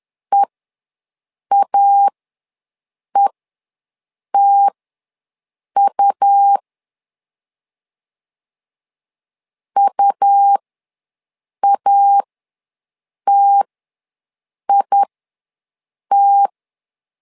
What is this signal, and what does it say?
Morse code, heard as a keyed tone, about 11 wpm, spaced as for 6 wpm, 800 Hz. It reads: EAETU UATIT
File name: LEZIONI / 2 UHDN / 2 U ws 10wpm 2017